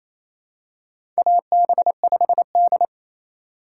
Morse code, heard as a keyed tone, about 29 wpm, 700 Hz. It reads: AB5D
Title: USA Call Signs - 4 Characters - 2x Character Spacing - 28wpm